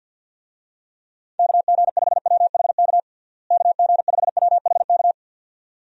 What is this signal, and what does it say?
Morse code, heard as a keyed tone, about 51 wpm, 700 Hz. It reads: KG5WHK KG5WHK
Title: USA Call Signs - 6 Characters - Repeated 2x - 50wpm